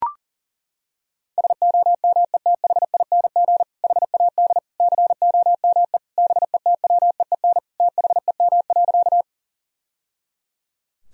Morse code, heard as a keyed tone, about 37 wpm, 700 Hz. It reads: SOMETHING HAD COME BETWEEN THEM.
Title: Sentences from Top 1000 Words - Review Part 1 - 40wpm